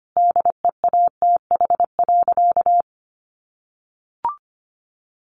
Morse code, heard as a keyed tone, about 25 wpm, 700 Hz. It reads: DEATH.
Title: Sentences from Top 1300 Words 25wpm